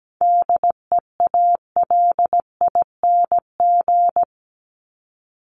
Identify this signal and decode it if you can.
Morse code, heard as a keyed tone, about 17 wpm, 700 Hz. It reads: DEALING
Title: ING Suffix - Top 100 Words with Base Words 17wpm